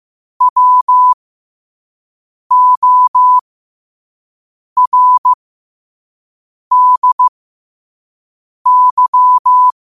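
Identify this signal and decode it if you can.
Morse code, heard as a keyed tone, about 15 wpm, spaced as for 5 wpm, 1000 Hz. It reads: WORDY
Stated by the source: Quote-of-the-day Morse code practice at 5 wpm.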